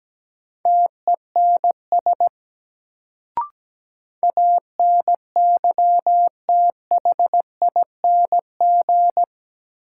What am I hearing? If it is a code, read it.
Morse code, heard as a keyed tone, about 17 wpm, 700 Hz. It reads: TENS ANYTHING
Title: Sets of 4 Words - New Word Drill - 17wpm